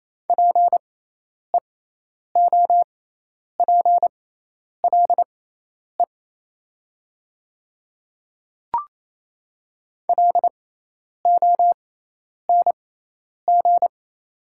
Morse code, heard as a keyed tone, about 27 wpm, spaced as for 10 wpm, 700 Hz. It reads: PEOPLE LONG
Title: 6x Character Spacing - Top 300 Words - 28wpm